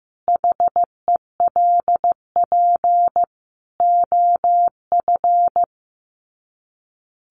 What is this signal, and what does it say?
Morse code, heard as a keyed tone, about 15 wpm, 700 Hz. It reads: HELP OF
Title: Sets of 4 Words 15wpm